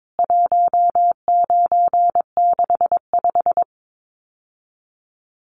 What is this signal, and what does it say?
Morse code, heard as a keyed tone, about 22 wpm, 700 Hz. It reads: 1965
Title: CWT Exchange 22wpm